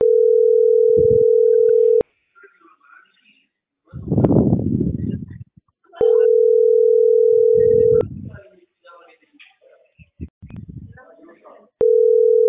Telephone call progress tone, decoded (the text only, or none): ringback tone